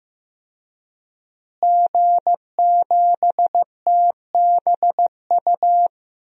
Morse code, heard as a keyed tone, about 15 wpm, 700 Hz. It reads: G7TBU
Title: DXCC-100 Call Signs - 5 Characters - Repeated 3x - 15wpm